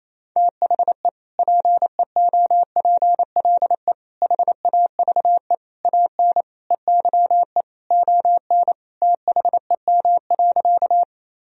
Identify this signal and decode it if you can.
Morse code, heard as a keyed tone, about 28 wpm, 700 Hz. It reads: THE PEOPLE HAVE AN EYE ON THEM.
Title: Sentences from Top 1000 Words - Review Part 1 - 28wpm